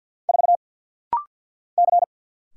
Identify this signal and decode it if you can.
Morse code, heard as a keyed tone, about 50 wpm, 700 Hz. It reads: 4 C
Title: Single Letter-Number - Rapid-Fire 50wpm